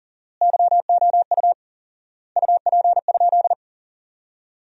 Morse code, heard as a keyed tone, about 40 wpm, 700 Hz. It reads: YOU UP?
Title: 4x Word-Spacing - Sentences from Top 500 Words - 40wpm